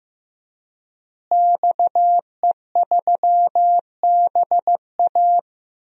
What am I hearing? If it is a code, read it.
Morse code, heard as a keyed tone, about 15 wpm, 700 Hz. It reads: XE3BA
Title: DXCC-100 Call Signs - 5 Characters - Repeated 3x - 15wpm